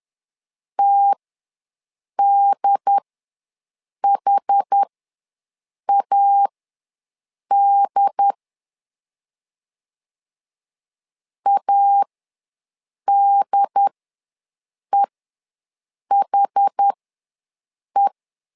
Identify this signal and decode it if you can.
Morse code, heard as a keyed tone, about 10 wpm, spaced as for 6 wpm, 800 Hz. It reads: TDHAD ADEHE
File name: LEZIONI / 2 UHDN / 2 UHD ws 10wpm 2058